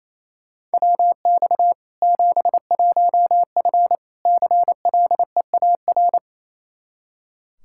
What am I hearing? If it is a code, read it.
Morse code, heard as a keyed tone, about 28 wpm, 700 Hz. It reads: WX 71F CLEAR
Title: K1USN SST QSO Practice 28wpm